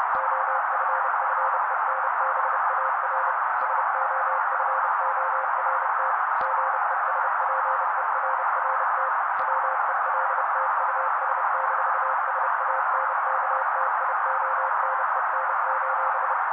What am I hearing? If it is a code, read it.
Morse code, heard as a keyed tone, about 30 wpm, 530 Hz. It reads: OFFENBAR SOFORT GESPERRT WELT US-SANKTIONEN:
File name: daily / 29wpm noise